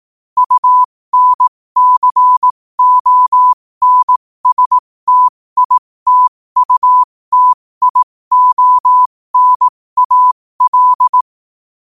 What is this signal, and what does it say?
Morse code, heard as a keyed tone, about 18 wpm, spaced as for 15 wpm, 1000 Hz. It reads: UNCONSTITUTIONAL
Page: qotd-at-15-wpm